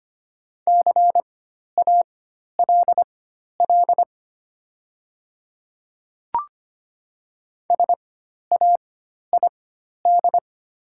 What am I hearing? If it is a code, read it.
Morse code, heard as a keyed tone, about 25 wpm, spaced as for 12 wpm, 700 Hz. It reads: CALL SAID
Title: 4x Character Spacing - Top 100 Words - 25wpm